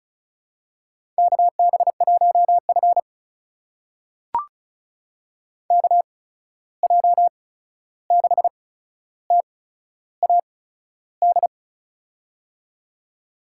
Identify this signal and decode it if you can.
Morse code, heard as a keyed tone, about 36 wpm, spaced as for 13 wpm, 700 Hz. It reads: KB1F KJ6TAD
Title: USA Call Signs - 8x Character Spacing - 35wpm